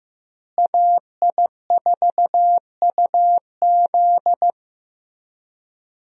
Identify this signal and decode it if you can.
Morse code, heard as a keyed tone, about 16 wpm, 700 Hz. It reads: AI4UZ
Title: USA Call Signs - 5 Characters - Repeated 3x - 15wpm